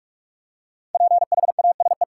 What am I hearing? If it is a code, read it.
Morse code, heard as a keyed tone, about 46 wpm, 700 Hz. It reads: PHASE
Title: Five Letter Words - 45wpm